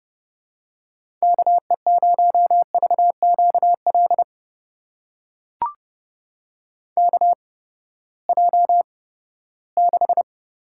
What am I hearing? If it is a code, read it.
Morse code, heard as a keyed tone, about 32 wpm, spaced as for 17 wpm, 700 Hz. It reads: KE0VQL KJ6
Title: USA Call Signs - 6 Characters - 8x Character Spacing - 30wpm